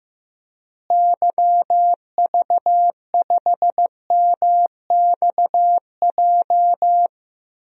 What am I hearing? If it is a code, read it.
Morse code, heard as a keyed tone, about 15 wpm, 700 Hz. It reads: YV5MXJ
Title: DXCC-100 Call Signs - 6 Characters - Repeated 2x - 15wpm